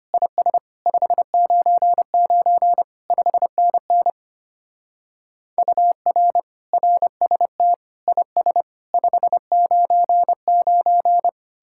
Morse code, heard as a keyed tone, about 32 wpm, 700 Hz. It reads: IS 599 5NN UR RST IS 599